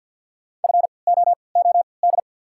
Morse code, heard as a keyed tone, about 54 wpm, 690 Hz. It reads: F O O D
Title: 3x Character Spacing - Top 300 Words - 50wpm